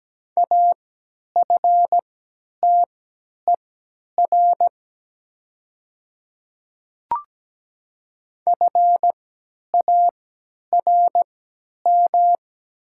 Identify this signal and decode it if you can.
Morse code, heard as a keyed tone, about 17 wpm, spaced as for 10 wpm, 700 Hz. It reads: AFTER FARM
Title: 3x Character Spacing - Top 500 Words - 17wpm